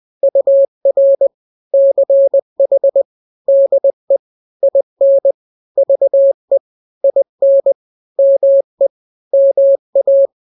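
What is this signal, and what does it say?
Morse code, heard as a keyed tone, about 20 wpm, 550 Hz. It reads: UR CH DE IN VE IN ME MA
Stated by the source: Morse drill of two-letter n-grams at 20/18 WPM (Farnsworth), from KK7RR.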